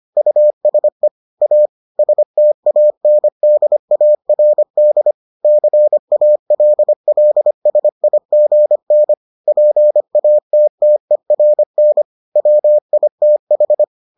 Morse code, heard as a keyed tone, about 25 wpm, 600 Hz. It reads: USE A STANDARD CALLSIGN PATTERN WITH